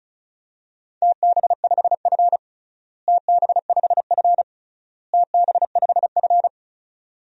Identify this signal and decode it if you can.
Morse code, heard as a keyed tone, about 36 wpm, 700 Hz. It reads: TB5F TB5F TB5F
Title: DXCC-100 Call Signs - 4 Characters - Repeated 3x - 35wpm